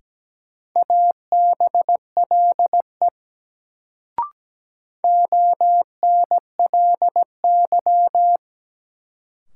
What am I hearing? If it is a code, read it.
Morse code, heard as a keyed tone, about 17 wpm, 700 Hz. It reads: ABLE ONLY
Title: Sets of 2 Words - New Word Drill - 17wpm